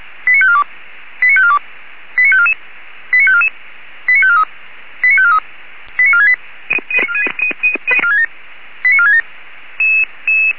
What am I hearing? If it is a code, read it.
Five-tone selective call (ZVEI), heard as quick runs of five tones, tones of about 70 ms. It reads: 89542 89542 89544 89544 89543 89543 89468 89467
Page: SDR/ZVEI